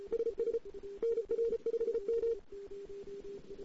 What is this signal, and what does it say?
Morse code, heard as a keyed tone, about 36 wpm, 460 Hz. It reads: SS NR5M
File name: CW / 2008-ss-cw-k5tr-op-n5rz -04:00